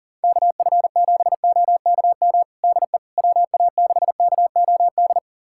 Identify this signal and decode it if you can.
Morse code, heard as a keyed tone, about 40 wpm, 700 Hz. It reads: KF7OKM DE WA6KOD
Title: USA Callsign DE Callsign - 6 Characters - 40wpm